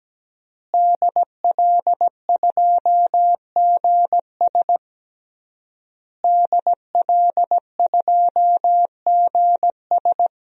Morse code, heard as a keyed tone, about 17 wpm, 700 Hz. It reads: DL2GS DL2GS
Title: DXCC-100 Call Signs - 5 Characters - Repeated 3x - 17wpm